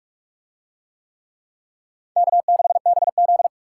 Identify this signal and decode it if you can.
Morse code, heard as a keyed tone, about 44 wpm, 700 Hz. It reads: K6BZ